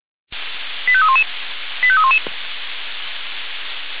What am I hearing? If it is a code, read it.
Five-tone selective call (ZVEI), heard as quick runs of five tones, tones of about 70 ms. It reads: 85311 85311
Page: ZVEI